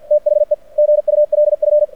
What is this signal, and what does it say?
Morse code, heard as a keyed tone, about 51 wpm, 600 Hz. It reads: THE GULF
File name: archived / CRSnet